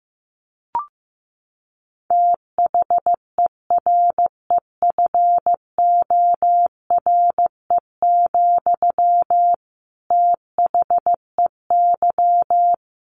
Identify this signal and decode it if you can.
Morse code, heard as a keyed tone, about 15 wpm, 700 Hz. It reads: THEREFORE, THEY
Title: Sentences from Top 800 Words 15wpm